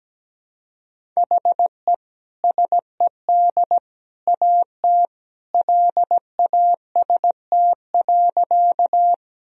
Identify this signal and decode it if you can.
Morse code, heard as a keyed tone, about 17 wpm, 700 Hz. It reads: HE SED AT LAST.
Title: Sentences from Top 300 Words and Abbreviations 17wpm